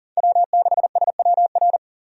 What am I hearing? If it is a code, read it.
Morse code, heard as a keyed tone, about 40 wpm, 700 Hz. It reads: W6SWR
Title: USA Call Signs - 5 Characters - 40wpm